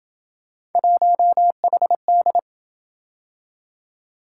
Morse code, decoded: 1HD

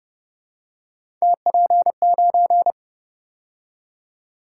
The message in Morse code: TP9

